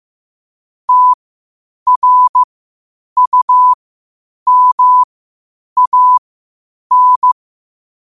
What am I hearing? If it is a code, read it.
Morse code, heard as a keyed tone, about 15 wpm, spaced as for 8 wpm, 1000 Hz. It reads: TRUMAN
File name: qotdfiles / 8wpm